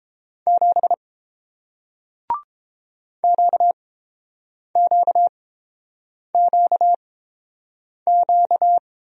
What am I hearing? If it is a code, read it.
Morse code, heard as a keyed tone, about 30 wpm, spaced as for 7 wpm, 700 Hz. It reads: 7 QQQQ